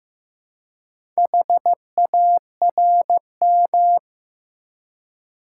Morse code, heard as a keyed tone, about 15 wpm, 700 Hz. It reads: HARM
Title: Four Letter Words - 15wpm